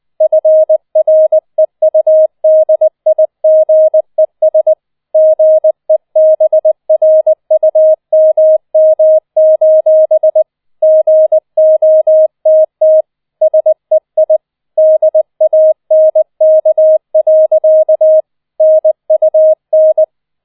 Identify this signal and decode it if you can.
Morse code, heard as a keyed tone, about 20 wpm, 630 Hz. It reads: FREUDIGES GEBRUMM: GOTT SEI DANK. NUN
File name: 007 Letzter Streich 20wpm